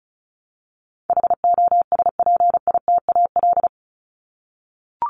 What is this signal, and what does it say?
Morse code, heard as a keyed tone, about 34 wpm, 700 Hz. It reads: HOSPITAL
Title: Top 1751-2000 Words 35wpm